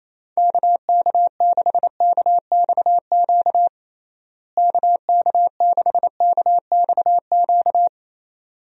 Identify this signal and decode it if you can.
Morse code, heard as a keyed tone, about 27 wpm, 700 Hz. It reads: KK6KXQ KK6KXQ
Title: USA Call Signs - 6 Characters - Repeated 2x - 28wpm